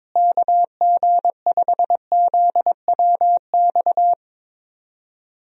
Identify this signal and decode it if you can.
Morse code, heard as a keyed tone, about 22 wpm, 700 Hz. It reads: KG5ZWX
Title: USA Call Signs - 6 Characters - Repeated 2x - 22wpm